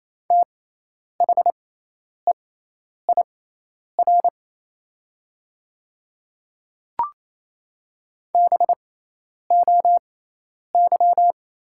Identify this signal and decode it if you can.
Morse code, heard as a keyed tone, about 28 wpm, spaced as for 10 wpm, 700 Hz. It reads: THEIR BOY
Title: 6x Character Spacing - Top 300 Words - 28wpm